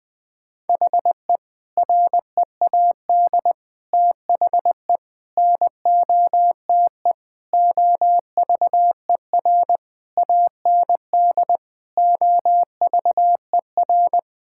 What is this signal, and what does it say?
Morse code, heard as a keyed tone, about 21 wpm, 700 Hz. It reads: HE READ THE NOTE OVER AND OVER
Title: Sentences with 4-Word Phrases - 20wpm